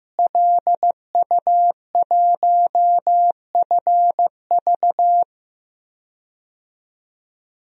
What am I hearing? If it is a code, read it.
Morse code, heard as a keyed tone, about 16 wpm, 700 Hz. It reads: LU1FV
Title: DXCC-100 Call Signs - 5 Characters - Repeated 3x - 15wpm